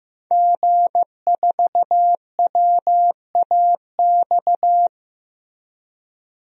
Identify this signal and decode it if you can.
Morse code, heard as a keyed tone, about 15 wpm, 700 Hz. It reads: G4WAX
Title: DXCC-100 Call Signs - 5 Characters - Repeated 3x - 15wpm